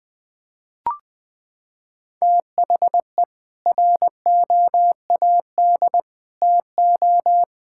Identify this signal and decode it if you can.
Morse code, heard as a keyed tone, about 20 wpm, 700 Hz. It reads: THE ROAD TO
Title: Sets of 8 Words - 20wpm